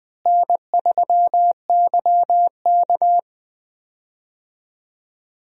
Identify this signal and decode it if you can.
Morse code, heard as a keyed tone, about 20 wpm, 700 Hz. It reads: N3YK
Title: USA Call Signs - 4 Characters - Repeated 2x - 20wpm